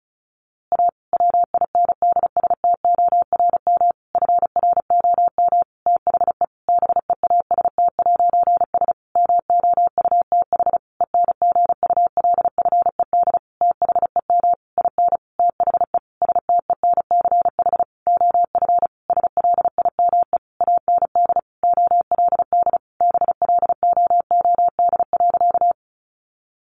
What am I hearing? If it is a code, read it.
Morse code, heard as a keyed tone, about 34 wpm, 700 Hz. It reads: A WINDSTORM FROM THE BEAST'S MOUTH ENGULFED THEM IN THE STENCH OF SLIME AND OLD BLOOD.
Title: A Sound Of Thunder 35wpm